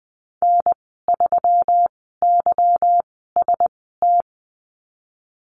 Morse code, heard as a keyed tone, about 20 wpm, spaced as for 15 wpm, 700 Hz. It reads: N3YST
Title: 173 Lesson 28g - Number 9 - Call Signs 20wpm